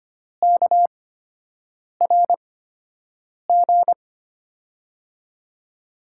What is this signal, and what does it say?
Morse code, heard as a keyed tone, about 25 wpm, spaced as for 7 wpm, 700 Hz. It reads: KRG
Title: USA Call Signs - 8x Character Spacing - 25wpm